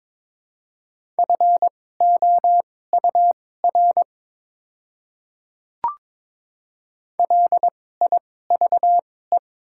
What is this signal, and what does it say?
Morse code, heard as a keyed tone, about 22 wpm, spaced as for 16 wpm, 700 Hz. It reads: FOUR LIVE
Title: 2x Character Spacing - Top 500 Words - 22wpm